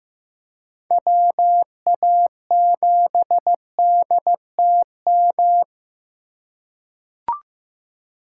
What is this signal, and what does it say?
Morse code, heard as a keyed tone, about 15 wpm, 700 Hz. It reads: WA7DTM
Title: USA Call Signs - 6 Characters - 2x Character Spacing - 15wpm